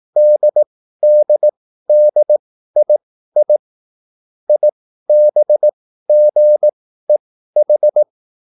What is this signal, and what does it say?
Morse code, heard as a keyed tone, about 18 wpm, spaced as for 13 wpm, 600 Hz. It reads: DDDII IBGEH